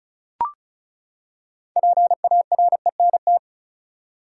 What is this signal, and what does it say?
Morse code, heard as a keyed tone, about 36 wpm, 700 Hz. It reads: PARENT
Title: Six Letter Words - 35wpm